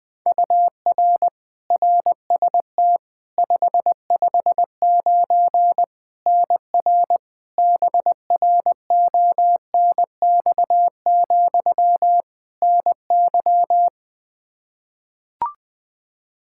Morse code, heard as a keyed tone, about 20 wpm, 700 Hz. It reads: UR RST 559 NR BRONX, NY
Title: Practice QSO - Exchange 1 - Beginning - 20wpm